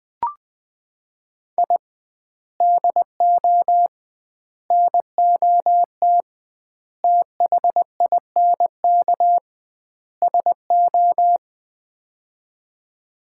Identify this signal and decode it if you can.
Morse code, heard as a keyed tone, about 20 wpm, 700 Hz. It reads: I DO NOT THINK SO
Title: 2x Word-Spacing - Sets of 5 Words - 20wpm